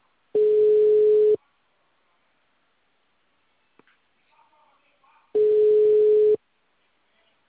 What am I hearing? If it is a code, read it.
Ringback tone (Europe, 425 Hz).
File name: out- -2015- - - .7579